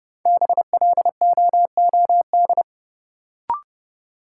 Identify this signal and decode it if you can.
Morse code, heard as a keyed tone, about 30 wpm, 700 Hz. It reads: BLOOD
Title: Sentences from Top 1000 Words - Review Part 2 - New Word Drill - 30wpm